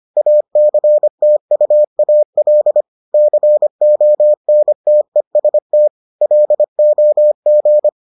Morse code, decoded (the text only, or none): ACTUAL CONTEST LOG